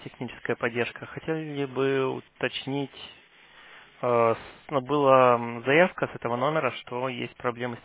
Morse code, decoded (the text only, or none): none